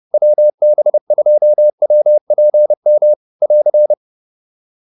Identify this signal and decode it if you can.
Morse code, heard as a keyed tone, about 30 wpm, 590 Hz. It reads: WB2WPM <AR>